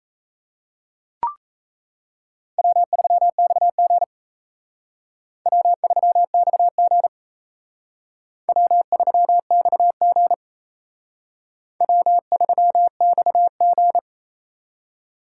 Morse code, heard as a keyed tone, about 35 wpm, 700 Hz. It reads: W3XG W3XG W3XG W3XG